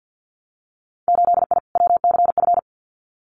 Morse code, heard as a keyed tone, about 50 wpm, 700 Hz. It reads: 8I WCF
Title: Winter Field Day Exchanges Encore 50wpm